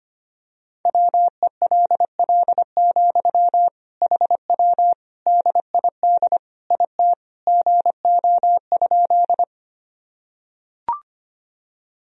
Morse code, heard as a keyed tone, about 25 wpm, 700 Hz. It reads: WELL, HW DID IT GO?
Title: Sentences from Top 200 Words and Abbreviations 25wpm